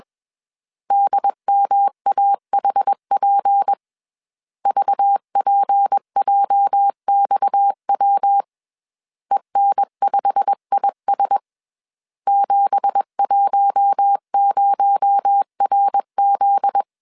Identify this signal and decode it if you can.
Morse code, heard as a keyed tone, about 21 wpm, 800 Hz. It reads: DMAHP VPJXW EN5IS 710RZ
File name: CW / DET / 1 9 1500C S C 20wpm 2310